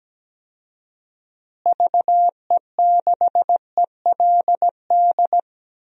Morse code, heard as a keyed tone, about 17 wpm, 700 Hz. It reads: VE6ELD
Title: DXCC-100 Call Signs - 6 Characters - Repeated 3x - 17wpm